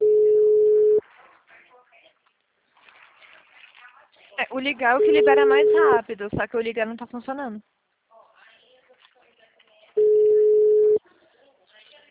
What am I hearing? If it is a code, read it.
Ringback tone (Europe, 425 Hz), with speech.
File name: out- -2005- - - .253